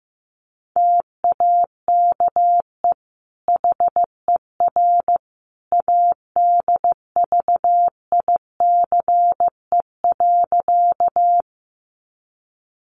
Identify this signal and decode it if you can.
Morse code, heard as a keyed tone, about 15 wpm, 700 Hz. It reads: TAKE HER ADVICE.
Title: Sentences from Top 1100 Words 15wpm